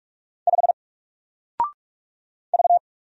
Morse code, heard as a keyed tone, about 45 wpm, 700 Hz. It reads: 5 V
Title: Single Letter-Number 45wpm